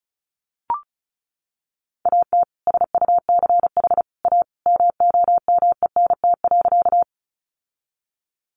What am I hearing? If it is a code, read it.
Morse code, heard as a keyed tone, about 34 wpm, 700 Hz. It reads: AT SUCH A MOMENT.
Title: Sentences from Top 1000 Words 35wpm